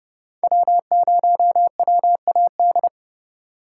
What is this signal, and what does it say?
Morse code, heard as a keyed tone, about 40 wpm, 700 Hz. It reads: W0WAD